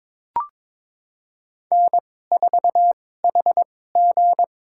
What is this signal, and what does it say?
Morse code, heard as a keyed tone, about 21 wpm, spaced as for 16 wpm, 700 Hz. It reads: N4HG
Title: USA Call Signs - 2x Character Spacing - 22wpm